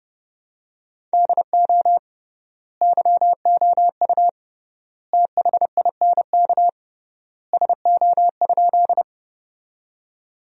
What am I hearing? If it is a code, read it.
Morse code, heard as a keyed tone, about 30 wpm, 700 Hz. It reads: DO YOU THINK SO?
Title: 3x Word-Spacing - Sentences from Top 200 Words - 30wpm